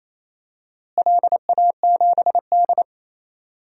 Morse code, heard as a keyed tone, about 28 wpm, 700 Hz. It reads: LA7D
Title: DXCC-100 Call Signs - 4 Characters - Repeated 2x - 28wpm